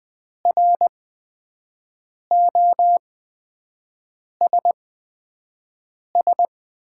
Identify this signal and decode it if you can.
Morse code, heard as a keyed tone, about 20 wpm, spaced as for 5 wpm, 700 Hz. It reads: ROSS
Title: 8x Character Spacing - Top 300 Words - 20wpm